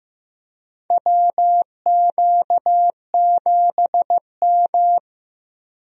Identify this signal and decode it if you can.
Morse code, heard as a keyed tone, about 15 wpm, 700 Hz. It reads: WQ7M
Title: USA Call Signs - 4 Characters - Repeated 3x - 15wpm